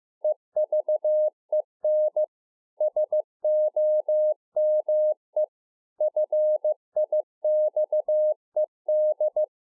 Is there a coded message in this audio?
Yes: Morse code from 0.2 to 9.4 s, heard as a keyed tone, about 15 wpm, 610 Hz. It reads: EVEN SOME FIXED